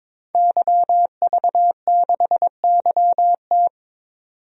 Morse code, heard as a keyed tone, about 22 wpm, 700 Hz. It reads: YV6YT
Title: DXCC-100 Call Signs - 5 Characters - 22wpm